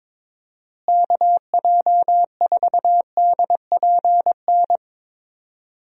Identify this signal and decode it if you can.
Morse code, heard as a keyed tone, about 22 wpm, 700 Hz. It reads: KJ4DPN
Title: USA Call Signs - 6 Characters - Repeated 3x - 22wpm